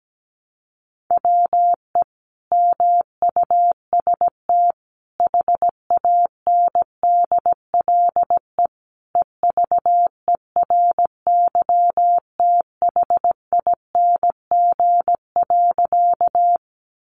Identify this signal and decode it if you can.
Morse code, heard as a keyed tone, about 17 wpm, 700 Hz. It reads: WE MUST HANDLE EVERYTHING.
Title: Sentences from Top 1100 Words 17wpm